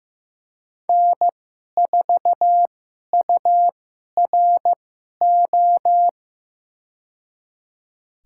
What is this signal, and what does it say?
Morse code, heard as a keyed tone, about 15 wpm, spaced as for 11 wpm, 700 Hz. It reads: N4URO
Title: USA Call Signs - 5 Characters - 2x Character Spacing - 15wpm